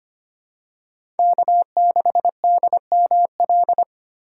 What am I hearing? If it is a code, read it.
Morse code, heard as a keyed tone, about 25 wpm, 700 Hz. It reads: K6DML
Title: USA Call Signs - 6x Character Spacing - 25wpm